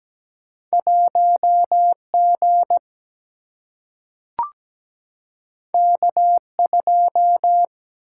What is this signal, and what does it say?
Morse code, heard as a keyed tone, about 17 wpm, 700 Hz. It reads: 1G K2
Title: Single-Letter Single-Number 17wpm